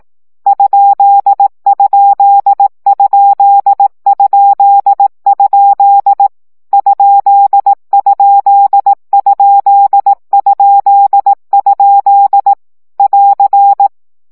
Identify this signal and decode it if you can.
Morse code, heard as a keyed tone, about 18 wpm, 800 Hz. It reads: ????? ????? <AR>
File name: CW gyakorlas kerdojel 15 baud